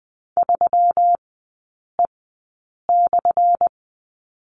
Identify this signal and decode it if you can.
Morse code, heard as a keyed tone, about 20 wpm, spaced as for 8 wpm, 700 Hz. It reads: 3E/
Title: 178 Lesson 29e - Slash - Three Characters 20wpm